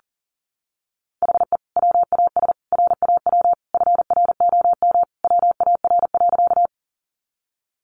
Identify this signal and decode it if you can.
Morse code, heard as a keyed tone, about 40 wpm, 700 Hz. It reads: HE WAS RAW FROM WAR.
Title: Sentences from Top 1500 Words 40wpm